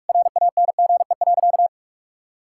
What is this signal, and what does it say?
Morse code, heard as a keyed tone, about 41 wpm, 700 Hz. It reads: RANGE.
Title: Sentences from Top 1000 Words - Review Part 6 - 45wpm